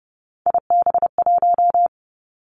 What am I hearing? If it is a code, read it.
Morse code, heard as a keyed tone, about 31 wpm, 700 Hz. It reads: IB1
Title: Three Challenging Letter-Number Combinations 30wpm